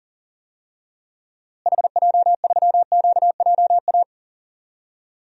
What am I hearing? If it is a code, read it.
Morse code, heard as a keyed tone, about 40 wpm, 700 Hz. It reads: HJ3QJA
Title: DXCC-100 Call Signs - 6 Characters - Repeated 2x - 40wpm